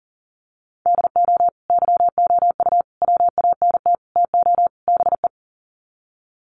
Morse code, heard as a keyed tone, about 40 wpm, 700 Hz. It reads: DO YOU WANT TO BE
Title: Sets of 5 Words Encore2 40wpm